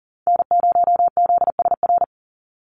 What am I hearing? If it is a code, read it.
Morse code, heard as a keyed tone, about 40 wpm, 700 Hz. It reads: N0ZSR
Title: USA Call Signs 40wpm